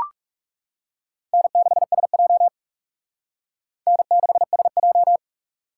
Morse code, heard as a keyed tone, about 39 wpm, 700 Hz. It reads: N6SJ N6SJ